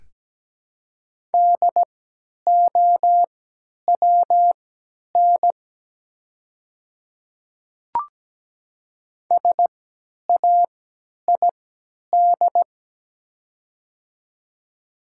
Morse code, heard as a keyed tone, about 17 wpm, spaced as for 10 wpm, 700 Hz. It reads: DOWN SAID
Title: 3x Character Spacing - Top 200 Words - 17wpm